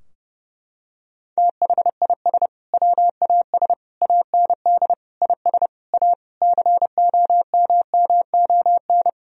Morse code, heard as a keyed tone, about 30 wpm, 700 Hz. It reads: THIS WAS AND IS A COMMON
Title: Sentences from Top 1000 Words - Review Part 4 - 30wpm